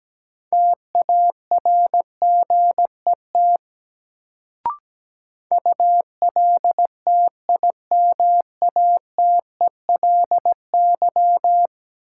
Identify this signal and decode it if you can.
Morse code, heard as a keyed tone, about 17 wpm, 700 Hz. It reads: TARGET ULTIMATELY
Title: Sentences from Top 1200 Words - New Word Drill - 17wpm